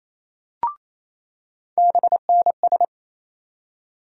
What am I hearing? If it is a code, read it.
Morse code, heard as a keyed tone, about 27 wpm, 700 Hz. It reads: BNS
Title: USA Call Sign Postfixes - 3 Letters 28wpm